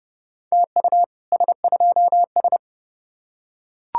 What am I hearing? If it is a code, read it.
Morse code, heard as a keyed tone, about 29 wpm, 700 Hz. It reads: TU S2S